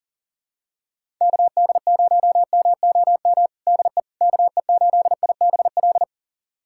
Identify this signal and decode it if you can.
Morse code, heard as a keyed tone, about 40 wpm, 700 Hz. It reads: KD0MOM DE KE8IBL
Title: USA Callsign DE Callsign - 6 Characters - 40wpm